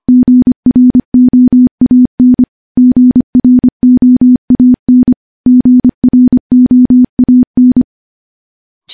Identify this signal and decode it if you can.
Morse code, heard as a keyed tone, about 25 wpm, 250 Hz. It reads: GROAN GROAN GROAN
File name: morse / morse-training-25WPM